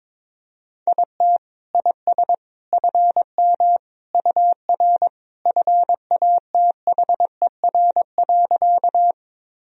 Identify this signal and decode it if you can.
Morse code, heard as a keyed tone, about 21 wpm, 700 Hz. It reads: IT IS FM UR FATHER.
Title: Sentences from Top 300 Words and Abbreviations 22wpm